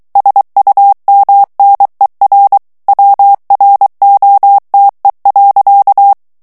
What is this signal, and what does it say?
Morse code, heard as a keyed tone, about 23 wpm, 800 Hz. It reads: SUMNER WROTE.
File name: ANGLAIS texte clair 3 vitesse 22 mots minute